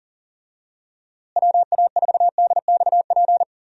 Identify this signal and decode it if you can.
Morse code, heard as a keyed tone, about 37 wpm, 700 Hz. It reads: WA4DXP
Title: USA Call Signs - 6 Characters - Repeated 3x - 40wpm